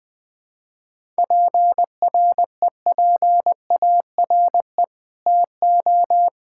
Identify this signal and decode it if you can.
Morse code, heard as a keyed tone, about 20 wpm, 700 Hz. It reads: PREPARE TO
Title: Sentences from Top 1000 Words - Review Part 6 - 20wpm